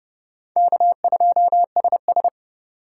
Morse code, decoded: K2SS

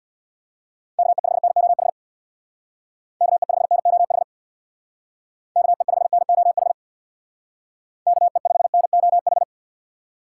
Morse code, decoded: KE5NOH KE5NOH KE5NOH KE5NOH

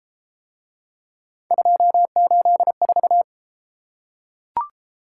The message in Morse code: 284

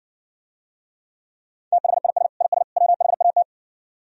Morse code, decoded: THIS IS WHAT

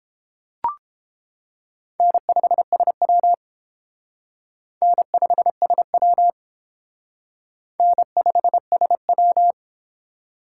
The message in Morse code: N5SW N5SW N5SW

